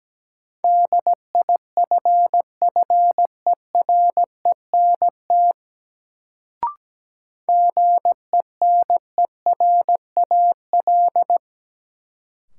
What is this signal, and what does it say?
Morse code, heard as a keyed tone, about 17 wpm, 700 Hz. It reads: DIFFERENT GENERAL
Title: Sentences from Top 600 Words - New Word Drill - 17wpm